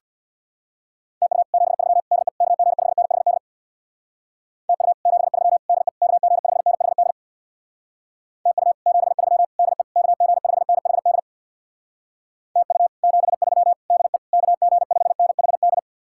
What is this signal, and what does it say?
Morse code, heard as a keyed tone, about 61 wpm, 700 Hz. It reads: TU 73 DE KG5NHD TU 73 DE KG5NHD TU 73 DE KG5NHD TU 73 DE KG5NHD
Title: Speed-Racing Common QSO Phrases - 50wpm